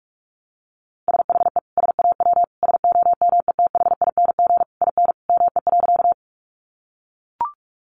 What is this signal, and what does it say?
Morse code, heard as a keyed tone, about 44 wpm, 700 Hz. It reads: SHE SAW SOMETHING IN ME.
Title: Sentences from Top 600 Words 45wpm